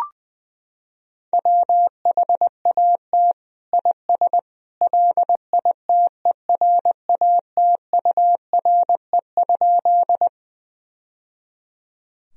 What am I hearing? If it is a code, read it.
Morse code, heard as a keyed tone, about 21 wpm, 700 Hz. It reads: WHAT IS LITERATURE?
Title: Sentences from Top 1000 Words - Review Part 3 - 20wpm